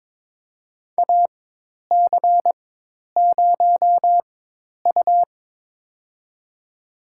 Morse code, heard as a keyed tone, about 22 wpm, spaced as for 10 wpm, 700 Hz. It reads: AC0U
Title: USA Call Signs - 4 Characters - 4x Character Spacing - 22wpm